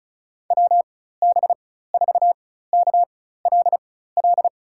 Morse code, heard as a keyed tone, about 36 wpm, spaced as for 16 wpm, 700 Hz. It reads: WB4KLL
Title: USA Call Signs - 6 Characters - 4x Character Spacing - 35wpm